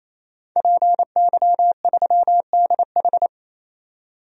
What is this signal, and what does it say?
Morse code, heard as a keyed tone, about 28 wpm, 700 Hz. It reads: PY3DH
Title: DXCC-100 Call Signs - 5 Characters - 28wpm